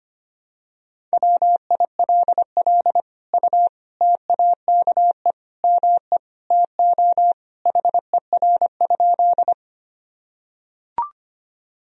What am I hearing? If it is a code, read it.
Morse code, heard as a keyed tone, about 25 wpm, 700 Hz. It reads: WILL U TAKE ME TO HER?
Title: Sentences from Top 200 Words and Abbreviations 25wpm